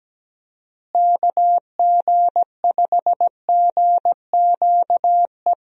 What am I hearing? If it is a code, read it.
Morse code, heard as a keyed tone, about 17 wpm, 700 Hz. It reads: KG5GQE
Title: USA Call Signs - 6 Characters - Repeated 3x - 17wpm